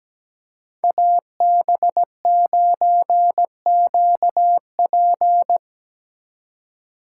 Morse code, heard as a keyed tone, about 17 wpm, 700 Hz. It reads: AB9QP
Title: USA Call Signs - 5 Characters - 3x Character Spacing - 17wpm